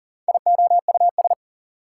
Morse code, heard as a keyed tone, about 40 wpm, 700 Hz. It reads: IOUS
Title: Prefixes and Suffixes 40wpm